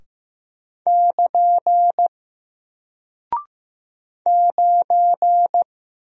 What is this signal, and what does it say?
Morse code, heard as a keyed tone, about 15 wpm, 700 Hz. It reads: <KN> 9
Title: Single Character (Letter:Number:Punctuation:Prosign) Alt 15wpm